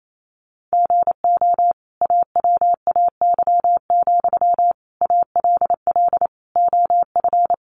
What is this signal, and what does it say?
Morse code, heard as a keyed tone, about 28 wpm, 700 Hz. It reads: GO AWAY, ALL OF
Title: Sentences from Top 600 Words 28wpm